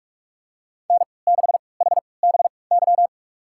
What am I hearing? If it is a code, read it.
Morse code, heard as a keyed tone, about 44 wpm, spaced as for 25 wpm, 700 Hz. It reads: N6HBY
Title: USA Call Signs - 3x Character Spacing - 45wpm